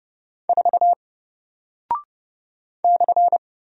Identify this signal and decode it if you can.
Morse code, heard as a keyed tone, about 30 wpm, 700 Hz. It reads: 4 /
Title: Single Character (Letter:Number:Punctuation:Prosign) Alt 30wpm